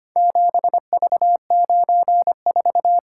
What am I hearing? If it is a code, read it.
Morse code, heard as a keyed tone, about 25 wpm, 700 Hz. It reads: 7V94
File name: practice / 25wpm / Four Letter-Number Combinations - 25wpm